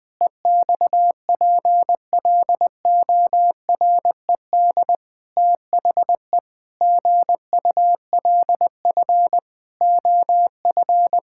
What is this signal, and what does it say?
Morse code, heard as a keyed tone, about 20 wpm, 700 Hz. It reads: EXPLORED THE GULF OF
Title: Pangram Challenge v2 - 20wpm